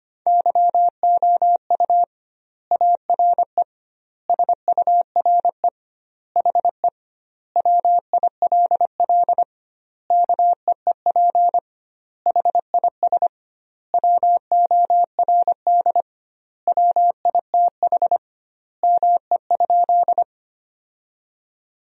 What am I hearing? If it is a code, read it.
Morse code, heard as a keyed tone, about 25 wpm, 700 Hz. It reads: YOU ARE SURE HE WILL KEEP HIS WORD WITH ME?
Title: 2x Word-Spacing - Sentences from Top 300 Words - 25wpm